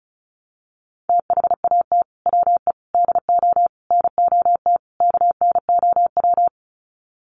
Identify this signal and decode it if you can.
Morse code, heard as a keyed tone, about 34 wpm, 700 Hz. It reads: THAT WE DO NOT KNOW
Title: Sets of 5 Words Encore2 35wpm